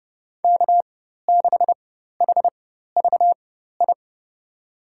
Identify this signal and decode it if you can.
Morse code, heard as a keyed tone, about 30 wpm, spaced as for 14 wpm, 700 Hz. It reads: K6HVI
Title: USA Call Signs - 5 Characters - 4x Character Spacing - 30wpm